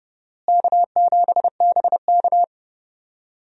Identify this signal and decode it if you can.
Morse code, heard as a keyed tone, about 30 wpm, 700 Hz. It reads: K7BK
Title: USA Call Signs - 4 Characters - Repeated 3x - 30wpm